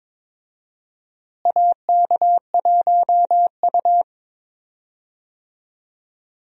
Morse code, heard as a keyed tone, about 22 wpm, 700 Hz. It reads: AK1U
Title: USA Call Signs - 4 Characters - Repeated 2x - 22wpm